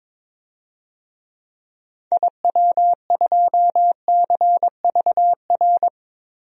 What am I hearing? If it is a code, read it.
Morse code, heard as a keyed tone, about 22 wpm, 700 Hz. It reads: IW2CVR
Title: DXCC-100 Call Signs - 6 Characters - 22wpm